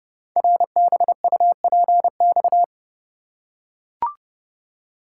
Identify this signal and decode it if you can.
Morse code, heard as a keyed tone, about 30 wpm, 700 Hz. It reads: RBUPX
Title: Code Groups - 5 Letters - 30wpm